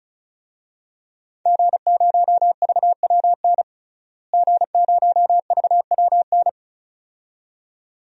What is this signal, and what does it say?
Morse code, heard as a keyed tone, about 36 wpm, 700 Hz. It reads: G0VWN G0VWN
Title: DXCC-100 Call Signs - 5 Characters - Repeated 2x - 35wpm